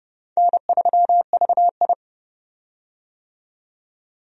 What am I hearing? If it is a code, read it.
Morse code, heard as a keyed tone, about 30 wpm, 700 Hz. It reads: N3VI
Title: USA Call Signs - 4 Characters - 30wpm